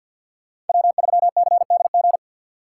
Morse code, heard as a keyed tone, about 53 wpm, 700 Hz. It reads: W3CDG
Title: USA Call Signs - 5 Characters - 3x Character Spacing - 50wpm